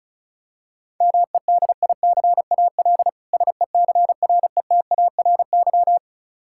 Morse code, heard as a keyed tone, about 36 wpm, 700 Hz. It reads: MEDICAL SECRETARY
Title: Practice QSO - Exchange 3 - More Info - 35wpm